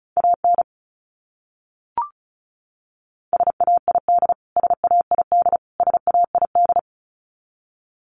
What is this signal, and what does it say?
Morse code, heard as a keyed tone, about 34 wpm, 700 Hz. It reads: AN SAID SAID SAID
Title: Top 500 Words Repeated 35wpm